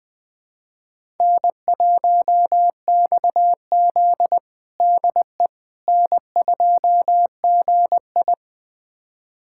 Morse code, heard as a keyed tone, about 20 wpm, 700 Hz. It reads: N1XZ DE N2GI
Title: USA Callsign DE Callsign - 4 Characters - 20wpm